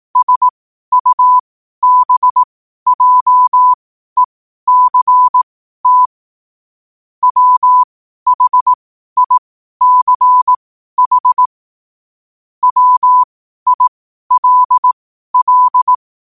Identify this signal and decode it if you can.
Morse code, heard as a keyed tone, about 18 wpm, spaced as for 12 wpm, 1000 Hz. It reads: SUBJECT WHICH WILL
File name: qotdfiles / 12wpm